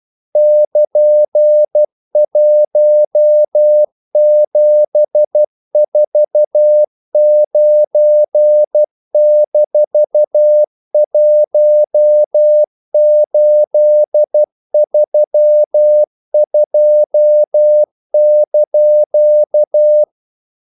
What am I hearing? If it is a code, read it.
Morse code, heard as a keyed tone, about 12 wpm, 600 Hz. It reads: <KN>1749-1832)
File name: CW / Practice / EnglishQuotations0312